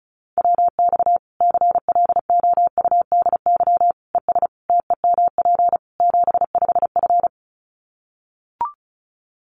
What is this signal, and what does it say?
Morse code, heard as a keyed tone, about 34 wpm, 700 Hz. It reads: WX CLOUDY ES TEMP 75F
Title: Common QSO Phrases 35wpm